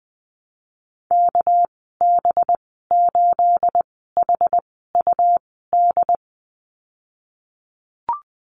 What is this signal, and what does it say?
Morse code, heard as a keyed tone, about 20 wpm, spaced as for 15 wpm, 700 Hz. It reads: KB8HUD
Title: 238 Lesson 38g - Letter q - Call Signs 20wpm